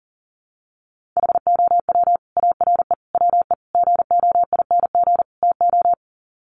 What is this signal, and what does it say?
Morse code, heard as a keyed tone, about 37 wpm, 700 Hz. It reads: HOW ARE WE GOING TO
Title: Sets of 5 Words Encore 40wpm